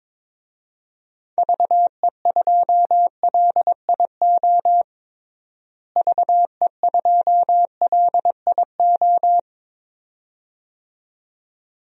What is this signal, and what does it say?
Morse code, heard as a keyed tone, about 22 wpm, 700 Hz. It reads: VE2LIO VE2LIO
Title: DXCC-100 Call Signs - 6 Characters - Repeated 2x - 22wpm